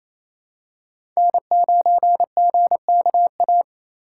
Morse code, heard as a keyed tone, about 28 wpm, 700 Hz. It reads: N9GKA
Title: USA Call Signs - 5 Characters - 4x Character Spacing - 28wpm